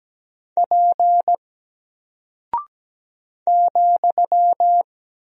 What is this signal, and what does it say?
Morse code, heard as a keyed tone, about 17 wpm, 700 Hz. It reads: P ,
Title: Single Character (Letter:Number:Punctuation:Prosign) Alt 17wpm